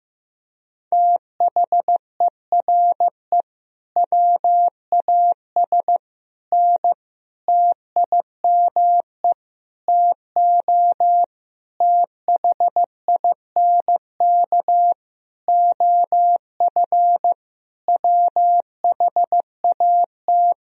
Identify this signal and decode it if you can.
Morse code, heard as a keyed tone, about 15 wpm, 700 Hz. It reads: THERE WAS N TIME TO THINK OF WHAT
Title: Sentences from Top 400 Words and Abbreviations 15wpm